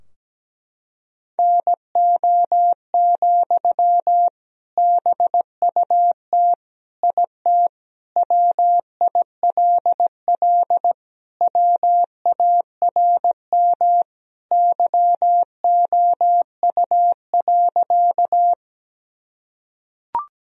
Morse code, heard as a keyed tone, about 17 wpm, 700 Hz. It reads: NO, BUT IT WILL WARM YOU.
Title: Sentences from Top 1000 Words - Review Part 2 - 17wpm